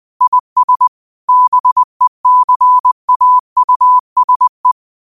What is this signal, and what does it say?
Morse code, heard as a keyed tone, about 20 wpm, 1000 Hz. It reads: IS BECAUSE
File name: qotdfiles / 20wpm